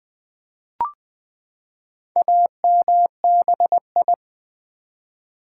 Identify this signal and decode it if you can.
Morse code, heard as a keyed tone, about 20 wpm, 700 Hz. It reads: AMBI
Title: Build A Word - 4 Letters At A Time - 8-Letter Words - 20wpm